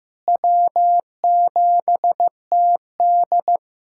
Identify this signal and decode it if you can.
Morse code, heard as a keyed tone, about 15 wpm, 700 Hz. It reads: W7TD
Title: USA Callsign DE Callsign - 4 Characters - 15wpm